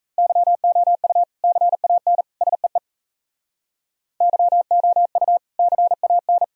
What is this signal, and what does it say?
Morse code, heard as a keyed tone, about 35 wpm, 700 Hz. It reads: YOU CAN SEE YOU CAN
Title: Speed-Racing - Sets of 3 Words 28wpm